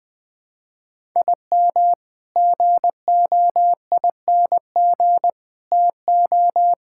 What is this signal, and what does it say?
Morse code, heard as a keyed tone, about 20 wpm, 700 Hz. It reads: IM GOING TO
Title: Sets of 8 Words - 20wpm